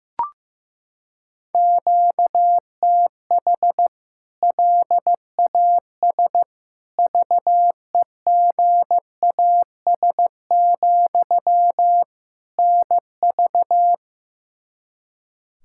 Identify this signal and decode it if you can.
Morse code, heard as a keyed tone, about 15 wpm, 700 Hz. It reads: QTH LAS VEGAS, NV
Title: K1USN SST QSO Practice 15wpm